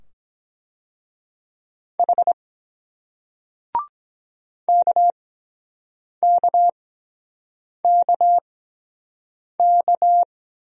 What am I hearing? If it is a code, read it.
Morse code, heard as a keyed tone, about 23 wpm, spaced as for 7 wpm, 700 Hz. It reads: H KKKK